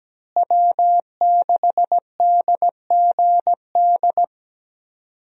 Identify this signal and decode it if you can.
Morse code, heard as a keyed tone, about 17 wpm, 700 Hz. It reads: W6DGD
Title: USA Call Signs - 5 Characters - Repeated 3x - 17wpm